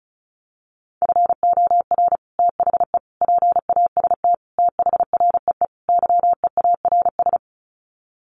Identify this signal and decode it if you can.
Morse code, heard as a keyed tone, about 34 wpm, 700 Hz. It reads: FOR THE PAST THREE YEARS
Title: Sets of 5 Words Encore2 35wpm